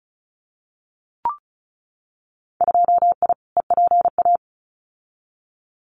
Morse code, heard as a keyed tone, about 34 wpm, 700 Hz. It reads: 2I EPA
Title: Winter Field Day Exchanges 35wpm